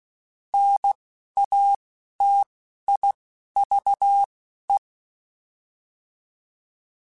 Morse code, heard as a keyed tone, about 16 wpm, spaced as for 11 wpm, 800 Hz. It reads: NATIVE